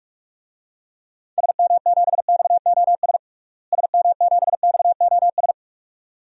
Morse code, heard as a keyed tone, about 42 wpm, 700 Hz. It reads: SM7XOS SM7XOS
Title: DXCC-100 Call Signs - 6 Characters - Repeated 2x - 45wpm